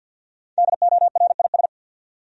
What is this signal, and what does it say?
Morse code, heard as a keyed tone, about 51 wpm, 700 Hz. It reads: DORIS
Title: USA Common Names - Max 5 Letters - 50wpm